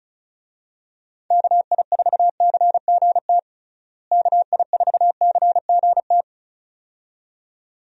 Morse code, heard as a keyed tone, about 34 wpm, 700 Hz. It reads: KI4CGT KI4CGT
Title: USA Call Signs - 6 Characters - Repeated 2x - 35wpm